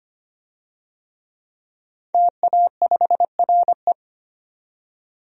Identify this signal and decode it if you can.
Morse code, heard as a keyed tone, about 25 wpm, 700 Hz. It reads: TA5RE
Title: DXCC-100 Call Signs - 5 Characters - 25wpm